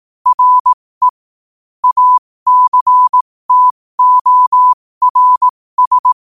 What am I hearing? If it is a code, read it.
Morse code, heard as a keyed tone, about 18 wpm, spaced as for 15 wpm, 1000 Hz. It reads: RE ACTORS
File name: qotdfiles / 15wpm